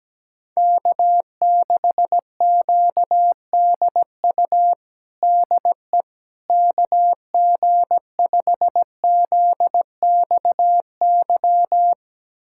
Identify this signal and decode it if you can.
Morse code, heard as a keyed tone, about 17 wpm, 700 Hz. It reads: K6QDU DE KG5ZXY
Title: USA Callsign DE Callsign 17wpm